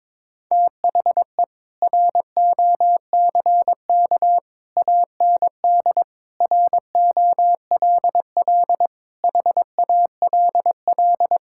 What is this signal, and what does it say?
Morse code, heard as a keyed tone, about 22 wpm, 700 Hz. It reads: THE ROCK AND ROLL HALL
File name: practice / 22wpm / Sets of 7 Words - 22wpm